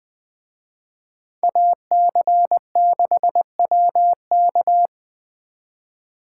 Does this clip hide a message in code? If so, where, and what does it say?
Yes: Morse code from 1.4 to 4.9 s, heard as a keyed tone, about 19 wpm, 700 Hz. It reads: AC6WK